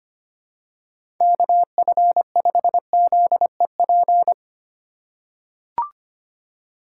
Morse code, heard as a keyed tone, about 25 wpm, 700 Hz. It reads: KF5ZEP
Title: USA Call Signs - 6 Characters - 3x Character Spacing - 25wpm